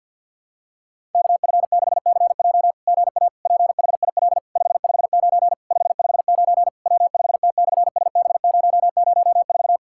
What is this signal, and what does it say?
Morse code, heard as a keyed tone, about 48 wpm, 690 Hz. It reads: KF6CJ GA PHIL 559 559 W5T/SB004
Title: SOTA QSO Practice 50wpm